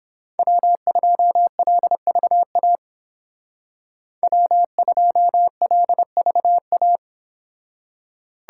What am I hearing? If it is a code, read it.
Morse code, heard as a keyed tone, about 30 wpm, 700 Hz. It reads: W2LVA W2LVA